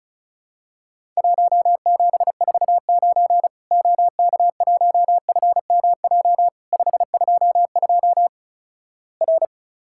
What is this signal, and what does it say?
Morse code, heard as a keyed tone, about 36 wpm, 700 Hz. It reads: 1749 OK1FMJ 522 R
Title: Worked All Europe DX Contest - QTC Exchange - 35wpm